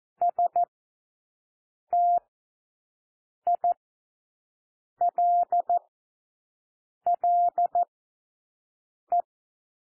Morse code, heard as a keyed tone, about 14 wpm, 700 Hz. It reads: S T I L L E
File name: cwmp3 / W1 SOUND WORDS 14 wpm 0